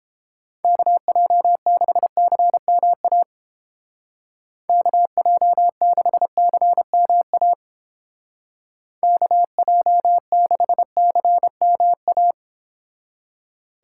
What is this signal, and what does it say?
Morse code, heard as a keyed tone, about 28 wpm, 700 Hz. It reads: KJ6CMA KJ6CMA KJ6CMA